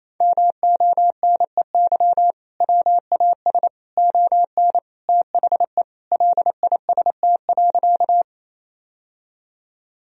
Morse code, heard as a keyed tone, about 27 wpm, 700 Hz. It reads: MONEY WAS ON THE LIST.
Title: Sentences from Top 1000 Words - Review Part 2 - 28wpm